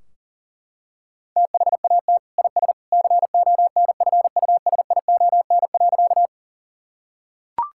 Morse code, heard as a keyed tone, about 40 wpm, 700 Hz. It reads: THAT IS CONFUSION.
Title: Sentences from Top 1600 Words 40wpm